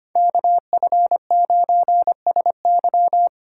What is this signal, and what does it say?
Morse code, heard as a keyed tone, about 31 wpm, 700 Hz. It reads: KF9SY
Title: Speed Racing - USA 2x2 Call Signs 25wpm